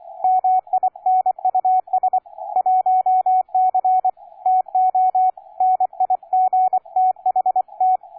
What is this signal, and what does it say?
Morse code, heard as a keyed tone, about 24 wpm, 740 Hz, with radio noise. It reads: MINUS 1C TO NIGTHT
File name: cw forum / K9DX 1818 le 17-12-2008 05h47